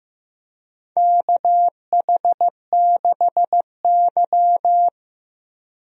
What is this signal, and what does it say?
Morse code, heard as a keyed tone, about 16 wpm, 700 Hz. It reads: KH6Y
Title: DXCC-100 Call Signs - 4 Characters - Repeated 2x - 15wpm